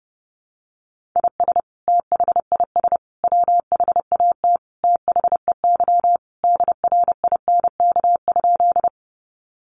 Morse code, heard as a keyed tone, about 30 wpm, 700 Hz. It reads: IS THIS WHAT THEY DRINK?
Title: Sentences from Top 1500 Words 30wpm